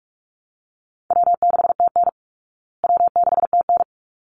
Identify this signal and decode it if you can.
Morse code, heard as a keyed tone, about 44 wpm, 700 Hz. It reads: W6TN W6TN
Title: Easy Call Signs Repeated 2x - 45wpm